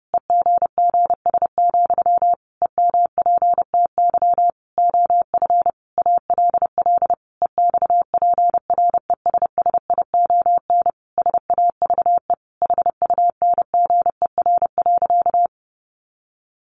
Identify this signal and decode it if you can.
Morse code, heard as a keyed tone, about 30 wpm, 700 Hz. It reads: EGGS, EMPTY OF ALL EXPRESSION SAVE HUNGER.
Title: A Sound Of Thunder 30wpm